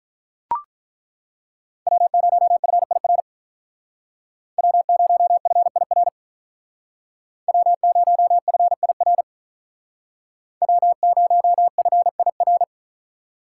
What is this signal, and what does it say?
Morse code, heard as a keyed tone, about 42 wpm, 700 Hz. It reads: W0FIR W0FIR W0FIR W0FIR
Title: Speed Racing - USA 1x3 Call Signs 35wpm